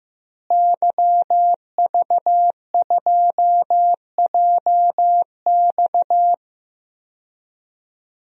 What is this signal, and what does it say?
Morse code, heard as a keyed tone, about 15 wpm, 700 Hz. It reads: YV2JX
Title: DXCC-100 Call Signs - 5 Characters - Repeated 3x - 15wpm